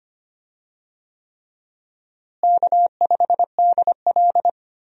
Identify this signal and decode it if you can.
Morse code, heard as a keyed tone, about 25 wpm, 700 Hz. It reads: K5DL
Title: USA Call Signs - 4 Characters - Repeated 3x - 25wpm